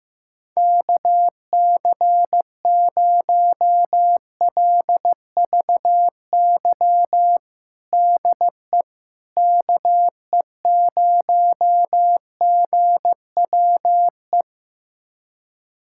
Morse code, heard as a keyed tone, about 15 wpm, 700 Hz. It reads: KC0LVY DE KE0GWE
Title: USA Callsign DE Callsign 15wpm